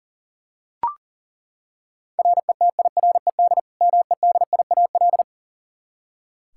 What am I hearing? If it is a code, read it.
Morse code, heard as a keyed tone, about 40 wpm, 700 Hz. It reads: RETIRED MEDIAL
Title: Practice QSO - Exchange 3 - More Info - 40wpm